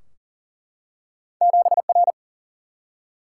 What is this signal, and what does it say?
Morse code, heard as a keyed tone, about 42 wpm, 700 Hz. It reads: 7R
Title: Single-Letter Single-Number 40wpm